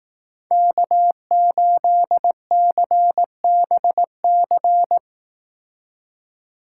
Morse code, heard as a keyed tone, about 18 wpm, 700 Hz. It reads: K8CBC
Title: Speed Racing - USA Call Signs 15wpm